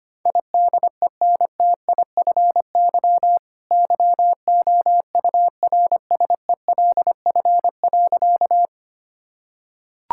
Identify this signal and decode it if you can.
Morse code, heard as a keyed tone, about 25 wpm, 700 Hz. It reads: IDENTIFY YOURSELF.
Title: Sentences from Top 1000 Words - Review Part 6 - 25wpm